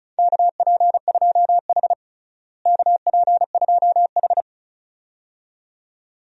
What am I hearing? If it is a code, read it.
Morse code, heard as a keyed tone, about 34 wpm, 700 Hz. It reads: KP2H KP2H
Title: DXCC-100 Call Signs - 4 Characters - Repeated 2x - 35wpm